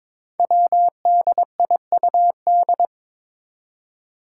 Morse code, heard as a keyed tone, about 23 wpm, 700 Hz. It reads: WDIUD
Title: Code Groups - 5 Letter:Numbers - 22wpm